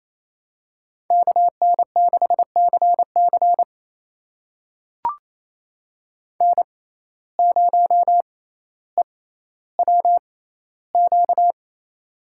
Morse code, heard as a keyed tone, about 28 wpm, spaced as for 14 wpm, 700 Hz. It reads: KN6CC N0EWQ
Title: USA Call Signs - 5 Characters - 6x Character Spacing - 28wpm